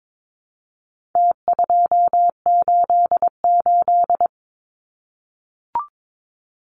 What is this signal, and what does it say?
Morse code, heard as a keyed tone, about 22 wpm, 700 Hz. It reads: T288
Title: Numbers - Contest Style - 4-Digit 22wpm